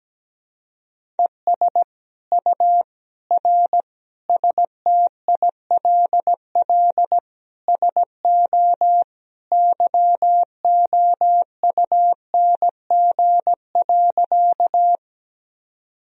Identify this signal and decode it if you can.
Morse code, heard as a keyed tone, about 17 wpm, 700 Hz. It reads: ES U R STILL SO YOUNG.
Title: Sentences from Top 400 Words and Abbreviations 17wpm